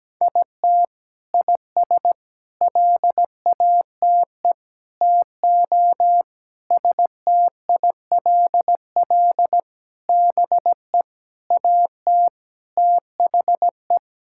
Morse code, heard as a keyed tone, about 17 wpm, 700 Hz. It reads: IT IS LATE TO STILL BE AT THE
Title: Sentences from Top 1000 Words - Review Part 5 - 17wpm